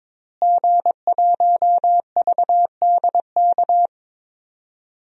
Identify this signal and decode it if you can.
Morse code, heard as a keyed tone, about 22 wpm, 700 Hz. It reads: G1VDK
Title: DXCC-100 Call Signs - 5 Characters - Repeated 3x - 22wpm